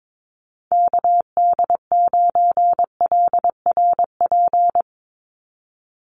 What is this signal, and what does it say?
Morse code, heard as a keyed tone, about 22 wpm, 700 Hz. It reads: KD9LRP
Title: USA Call Signs 22wpm